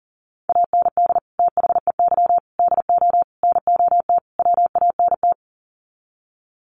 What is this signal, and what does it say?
Morse code, heard as a keyed tone, about 40 wpm, 700 Hz. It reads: AND THEY DO NOT WANT
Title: Sets of 5 Words Encore2 40wpm